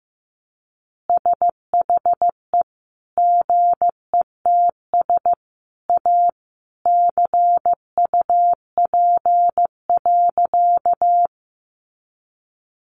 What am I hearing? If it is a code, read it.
Morse code, heard as a keyed tone, about 15 wpm, 700 Hz. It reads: SHE GETS A CUP.
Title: Easy Phrases 15wpm